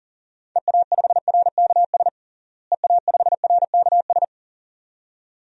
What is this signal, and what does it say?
Morse code, heard as a keyed tone, about 37 wpm, 700 Hz. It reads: EA5RKS EA5RKS
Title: DXCC-100 Call Signs - 6 Characters - Repeated 2x - 40wpm